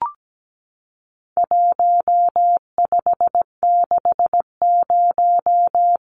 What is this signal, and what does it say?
Morse code, heard as a keyed tone, about 17 wpm, 700 Hz. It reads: 1560
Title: Numbers - Contest Style - 4-Digit 17wpm